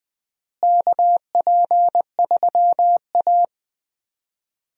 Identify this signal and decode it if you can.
Morse code, heard as a keyed tone, about 20 wpm, 700 Hz. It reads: KP3A
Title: USA Call Signs - 4 Characters - Repeated 2x - 20wpm